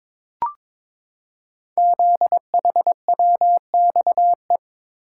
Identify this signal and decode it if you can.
Morse code, heard as a keyed tone, about 23 wpm, 700 Hz. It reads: ZHWXE
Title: Code Groups - 5 Letters - 22wpm